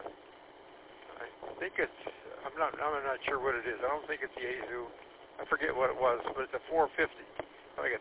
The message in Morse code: none

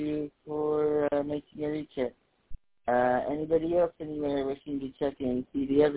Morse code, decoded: none

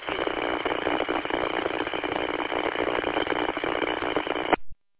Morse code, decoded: none

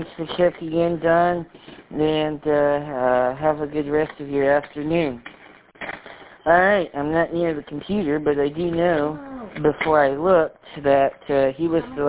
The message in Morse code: none

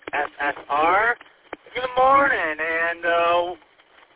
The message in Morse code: none

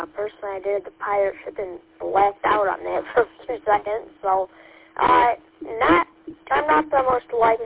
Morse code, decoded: none

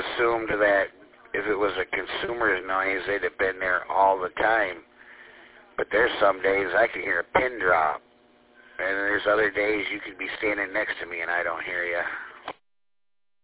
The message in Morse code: none

